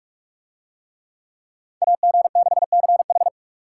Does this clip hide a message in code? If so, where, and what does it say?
Yes: Morse code from 1.8 to 3.3 s, heard as a keyed tone, about 44 wpm, 700 Hz. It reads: AG6CH